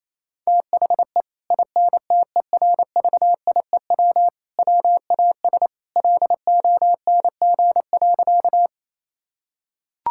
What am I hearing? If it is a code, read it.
Morse code, heard as a keyed tone, about 28 wpm, 700 Hz. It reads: THE INTERVIEW WAS LONG.
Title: Sentences from Top 1600 Words 28wpm